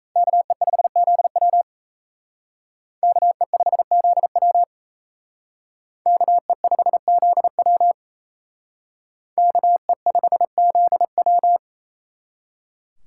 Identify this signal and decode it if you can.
Morse code, heard as a keyed tone, about 34 wpm, 700 Hz. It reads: KE5ZW KE5ZW KE5ZW KE5ZW
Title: Speed Racing - USA 2x2 Call Signs 28wpm